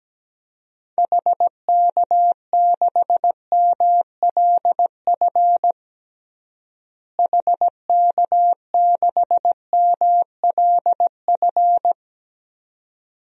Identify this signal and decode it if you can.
Morse code, heard as a keyed tone, about 17 wpm, 700 Hz. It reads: HK6MLF HK6MLF
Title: DXCC-100 Call Signs - 6 Characters - Repeated 2x - 17wpm